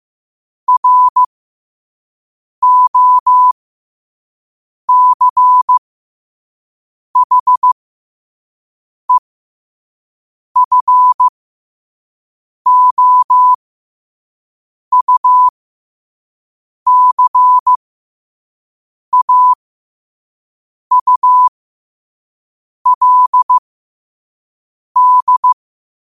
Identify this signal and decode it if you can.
Morse code, heard as a keyed tone, about 15 wpm, spaced as for 5 wpm, 1000 Hz. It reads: ROCHEFOUCAULD
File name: qotdfiles / 5wpm